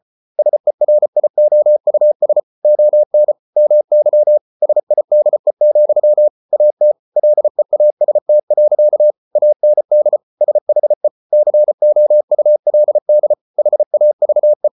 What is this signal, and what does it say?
Morse code, heard as a keyed tone, about 34 wpm, 600 Hz. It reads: SERIOUS ON MY SIDE, AT LEAST. AND SHE COULD HAVE